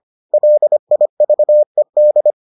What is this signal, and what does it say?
Morse code, heard as a keyed tone, about 25 wpm, 600 Hz. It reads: LIVED